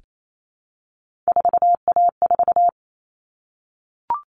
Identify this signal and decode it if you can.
Morse code, heard as a keyed tone, about 28 wpm, 700 Hz. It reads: 4A4
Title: All Cut Numbers - 3-Digit 28wpm